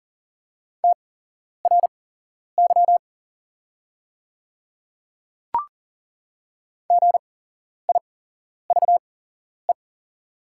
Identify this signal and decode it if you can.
Morse code, heard as a keyed tone, about 40 wpm, spaced as for 11 wpm, 700 Hz. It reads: TRY GIVE
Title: 8x Character Spacing - Top 500 Words - 40wpm